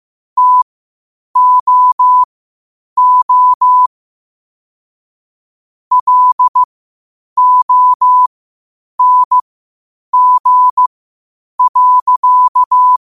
Morse code, heard as a keyed tone, about 15 wpm, spaced as for 8 wpm, 1000 Hz. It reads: TOO LONG.
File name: qotdfiles / 8wpm